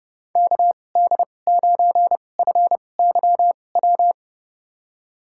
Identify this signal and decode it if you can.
Morse code, heard as a keyed tone, about 31 wpm, spaced as for 22 wpm, 700 Hz. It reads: KD9FYW
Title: USA Call Signs - 2x Character Spacing - 30wpm